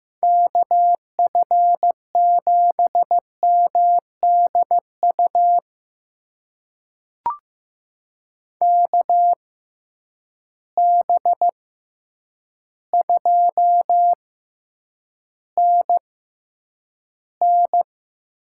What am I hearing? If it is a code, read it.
Morse code, heard as a keyed tone, about 15 wpm, spaced as for 8 wpm, 700 Hz. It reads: KF7MDU KB2NN
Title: USA Call Signs - 6 Characters - 6x Character Spacing - 15wpm